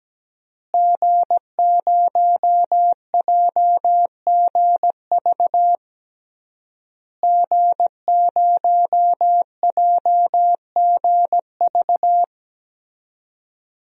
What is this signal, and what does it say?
Morse code, heard as a keyed tone, about 17 wpm, 700 Hz. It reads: G0JGV G0JGV
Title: DXCC-100 Call Signs - 5 Characters - Repeated 3x - 17wpm